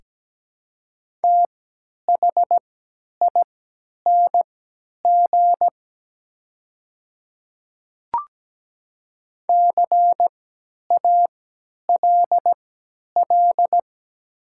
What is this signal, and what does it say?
Morse code, heard as a keyed tone, about 17 wpm, spaced as for 10 wpm, 700 Hz. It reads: THING CALL
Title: 3x Character Spacing - Top 100 Words - 17wpm